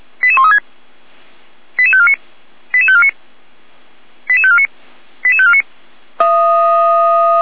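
Five-tone selective call (ZVEI), heard as quick runs of five tones, tones of about 70 ms. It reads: 88136 88549 88549 88549 88549
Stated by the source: ZVEI five-tone selective call received by SDR.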